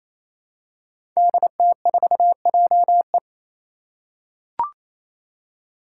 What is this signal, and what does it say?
Morse code, heard as a keyed tone, about 27 wpm, 700 Hz. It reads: DT4JE
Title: Five Challenging Letter-Number Combinations 28wpm